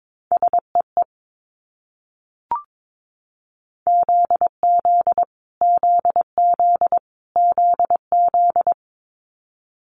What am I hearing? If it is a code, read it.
Morse code, heard as a keyed tone, about 22 wpm, 700 Hz. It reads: SEE ZZ ZZ ZZ
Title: Multi-letter Phonemes Repeated 3x 22wpm